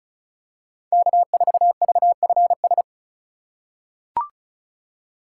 Morse code, heard as a keyed tone, about 36 wpm, 700 Hz. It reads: K4VFS
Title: USA Call Signs - 5 Characters - 2x Character Spacing - 35wpm